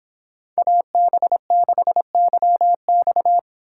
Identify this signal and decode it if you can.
Morse code, heard as a keyed tone, about 30 wpm, 700 Hz. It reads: AB6YX